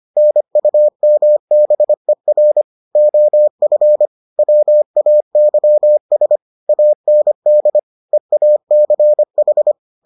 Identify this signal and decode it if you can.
Morse code, heard as a keyed tone, about 25 wpm, 600 Hz. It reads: NUMBER OF WAYS AND EACH